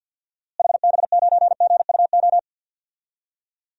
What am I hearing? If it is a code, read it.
Morse code, heard as a keyed tone, about 47 wpm, 700 Hz. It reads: HB9GUO
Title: DXCC-100 Call Signs - 6 Characters - Repeated 3x - 50wpm